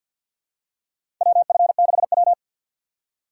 Morse code, heard as a keyed tone, about 51 wpm, 700 Hz. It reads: WF6W
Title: USA Call Signs - 4 Characters - 3x Character Spacing - 50wpm